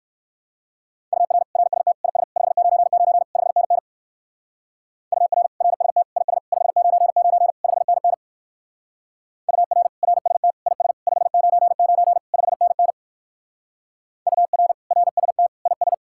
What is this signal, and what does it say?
Morse code, heard as a keyed tone, about 58 wpm, 700 Hz. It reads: UR RST IS 599 5NN UR RST IS 599 5NN UR RST IS 599 5NN UR RST IS